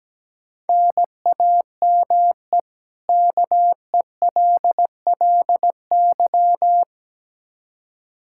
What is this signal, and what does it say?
Morse code, heard as a keyed tone, about 17 wpm, 700 Hz. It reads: NAME KELLY
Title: Practice QSO - Exchange 1 - Beginning - 17wpm